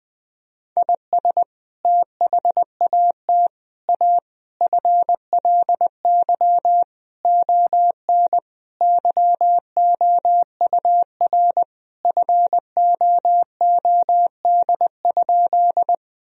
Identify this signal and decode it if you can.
Morse code, heard as a keyed tone, about 20 wpm, 700 Hz. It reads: IS THAT A FLY ON YOUR FOOD?
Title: Sentences from Top 1000 Words - Review Part 2 - 20wpm